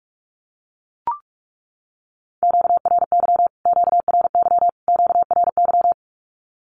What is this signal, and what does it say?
Morse code, heard as a keyed tone, about 44 wpm, 700 Hz. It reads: QRY QRY QRY
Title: Q-Codes Repeated 3x 45wpm